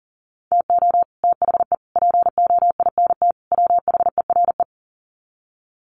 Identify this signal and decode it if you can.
Morse code, heard as a keyed tone, about 40 wpm, 700 Hz. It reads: TO THE POINT WHERE
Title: Sets of 4 Words 40wpm